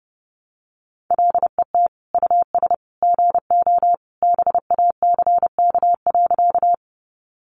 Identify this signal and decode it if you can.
Morse code, heard as a keyed tone, about 30 wpm, 700 Hz. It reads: LET US GO BACK.
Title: Sentences from Top 300 Words 30wpm